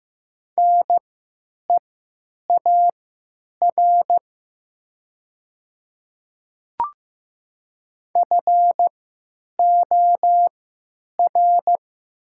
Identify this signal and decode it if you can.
Morse code, heard as a keyed tone, about 15 wpm, spaced as for 9 wpm, 700 Hz. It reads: NEAR FOR
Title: 3x Character Spacing - Top 200 Words - 15wpm